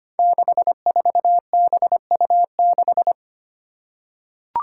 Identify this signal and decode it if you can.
Morse code, heard as a keyed tone, about 25 wpm, 700 Hz. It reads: 64BU6
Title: Five Challenging Letter-Number Combinations 25wpm